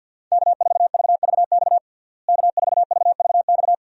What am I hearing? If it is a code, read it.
Morse code, heard as a keyed tone, about 49 wpm, 700 Hz. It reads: K4VVX K4VVX
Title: USA Call Signs - 5 Characters - Repeated 3x - 50wpm